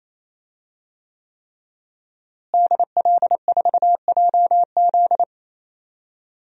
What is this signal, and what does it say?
Morse code, heard as a keyed tone, about 28 wpm, 700 Hz. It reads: DL4JZ
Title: DXCC-100 Call Signs - 5 Characters - Repeated 3x - 28wpm